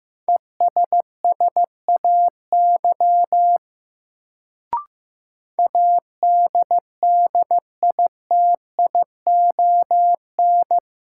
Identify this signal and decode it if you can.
Morse code, heard as a keyed tone, about 15 wpm, 700 Hz. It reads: ESSAY ADDITION
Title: Sentences from Top 1500 Words - New Word Drill Alt - 15wpm